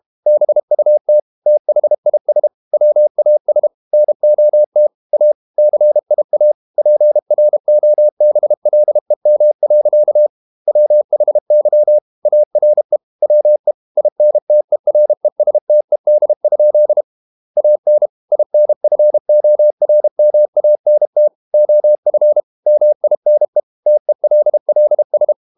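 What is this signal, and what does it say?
Morse code, heard as a keyed tone, about 32 wpm, 600 Hz. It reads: BUT THIS WAS NOT A CIA PROBLEM. WHY ARE WE INTERESTED? AN INFORMANT OF MINE TELLS
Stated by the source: Morse code practice at 32 wpm from CW med Gnister, reading Ken Follett's novel Never in English.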